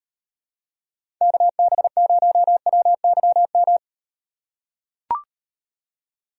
Morse code, heard as a keyed tone, about 38 wpm, 700 Hz. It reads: KB0WYM